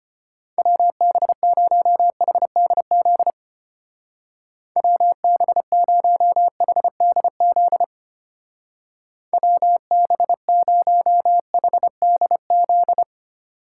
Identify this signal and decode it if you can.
Morse code, heard as a keyed tone, about 33 wpm, 700 Hz. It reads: WB0HDZ WB0HDZ WB0HDZ